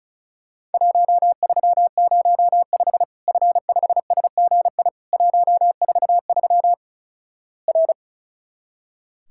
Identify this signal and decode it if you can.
Morse code, heard as a keyed tone, about 36 wpm, 700 Hz. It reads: 1305 F5SGI 143 R
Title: Worked All Europe DX Contest - QTC Exchange - 35wpm